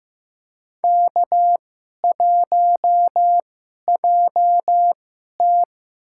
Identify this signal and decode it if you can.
Morse code, heard as a keyed tone, about 15 wpm, spaced as for 11 wpm, 700 Hz. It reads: K1JT
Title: USA Call Signs - 2x Character Spacing - 15wpm